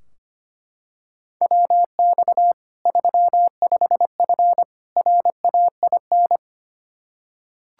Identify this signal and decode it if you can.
Morse code, heard as a keyed tone, about 25 wpm, 700 Hz. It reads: WX 35F RAIN
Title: K1USN SST QSO Practice 25wpm